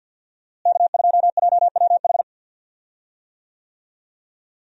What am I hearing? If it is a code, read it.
Morse code, heard as a keyed tone, about 50 wpm, 700 Hz. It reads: K2JWH
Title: USA Call Signs - 5 Characters - Repeated 2x - 50wpm